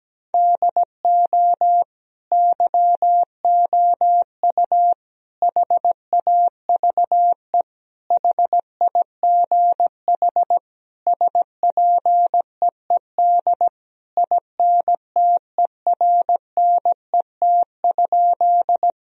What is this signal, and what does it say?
Morse code, heard as a keyed tone, about 17 wpm, 700 Hz. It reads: DO YOU HAVE HIGH SPEED INTERNET?
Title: Sentences from Top 1000 Words - Review Part 4 - 17wpm